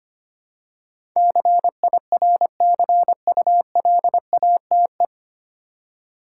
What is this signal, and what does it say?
Morse code, heard as a keyed tone, about 25 wpm, 700 Hz. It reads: CIRCULATE
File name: Nine Letter Words - 25wpm